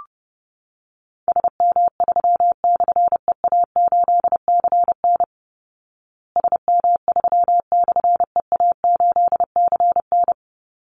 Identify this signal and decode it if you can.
Morse code, heard as a keyed tone, about 30 wpm, 700 Hz. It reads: SM3/EA8CN SM3/EA8CN
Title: Call Signs Repeated 2x - 30wpm